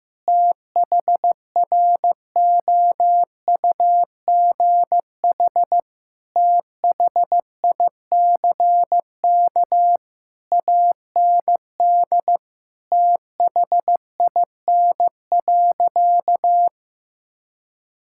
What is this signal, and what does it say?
Morse code, heard as a keyed tone, about 15 wpm, 700 Hz. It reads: THROUGH THICK AND THIN.
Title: Sentences from Top 1600 Words 15wpm